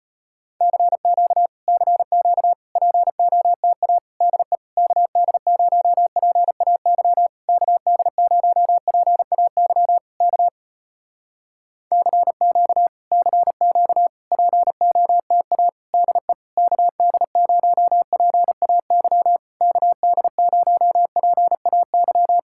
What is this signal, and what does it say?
Morse code, heard as a keyed tone, about 35 wpm, 700 Hz. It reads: CQ CQ POTA DE KD0PAY KD0PAY K CQ CQ POTA DE KD0PAY KD0PAY